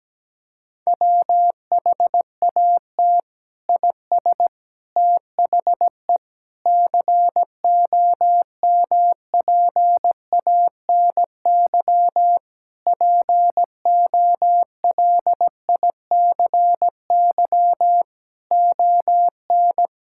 Morse code, heard as a keyed tone, about 17 wpm, 700 Hz. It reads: WHAT IS THE COMPANY POLICY ON
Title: Sentences from Top 1000 Words - Review Part 5 - 17wpm